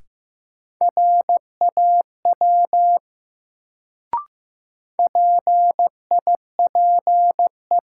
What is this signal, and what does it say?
Morse code, heard as a keyed tone, about 15 wpm, 700 Hz. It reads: RAW PIPE
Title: Sentences from Top 1500 Words - New Word Drill - 15wpm